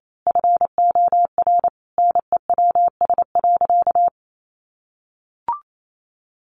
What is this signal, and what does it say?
Morse code, heard as a keyed tone, about 28 wpm, 700 Hz. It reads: FOR NEWS.
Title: Sentences from Top 800 Words 28wpm